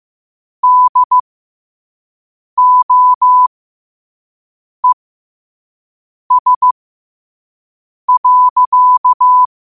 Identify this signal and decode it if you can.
Morse code, heard as a keyed tone, about 15 wpm, spaced as for 5 wpm, 1000 Hz. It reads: DOES.
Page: Quotes for Tue, 12 Aug 2025 in Morse Code at 5 words per minute.